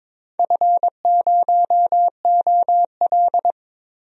Morse code, heard as a keyed tone, about 22 wpm, 700 Hz. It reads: F0OL
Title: DXCC-100 Call Signs - 4 Characters - 22wpm